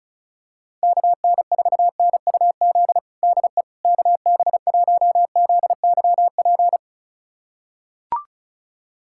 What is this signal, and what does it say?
Morse code, heard as a keyed tone, about 36 wpm, 700 Hz. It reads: KN4NUZ DE KB1ZYP
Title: USA Callsign DE Callsign 35wpm